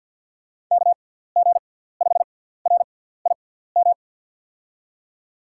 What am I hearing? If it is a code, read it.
Morse code, heard as a keyed tone, about 52 wpm, spaced as for 17 wpm, 700 Hz. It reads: KG5RIM
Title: USA Call Signs - 6 Characters - 6x Character Spacing - 50wpm